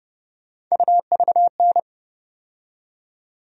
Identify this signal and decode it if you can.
Morse code, heard as a keyed tone, about 32 wpm, 700 Hz. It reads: UVN
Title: USA Call Sign Postfixes - 3 Letters 30wpm